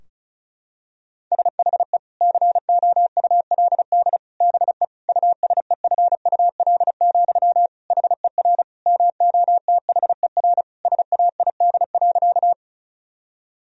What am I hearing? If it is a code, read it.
Morse code, heard as a keyed tone, about 36 wpm, 700 Hz. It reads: SHE COULD BE USEFUL, HER MOTHER SAID.
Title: Sentences from Top 1000 Words - Review Part 4 - 35wpm